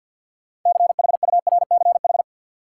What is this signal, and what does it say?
Morse code, heard as a keyed tone, about 48 wpm, 700 Hz. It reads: CHURCH
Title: Sets of 5 Words - New Word Drill - 50wpm